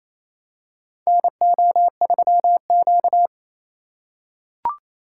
Morse code, heard as a keyed tone, about 28 wpm, 700 Hz. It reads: NO3Q
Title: USA Call Signs - 4x Character Spacing - 28wpm